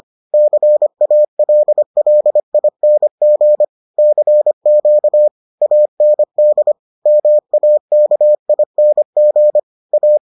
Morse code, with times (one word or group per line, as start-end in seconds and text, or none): 0.3-3.6 CALLING
4.0-5.3 CQ
5.6-6.7 AND
7.1-9.6 MAKING
9.9-10.2 A